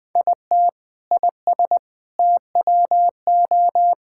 Morse code, heard as a keyed tone, about 21 wpm, 700 Hz. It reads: IT IS TWO
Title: Sentences with 4-Word Phrases - 20wpm